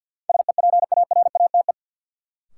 Morse code, heard as a keyed tone, about 53 wpm, 700 Hz. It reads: SEPARATE
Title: Sentences from Top 1000 Words - Review Part 7 - New Word Drill - 50wpm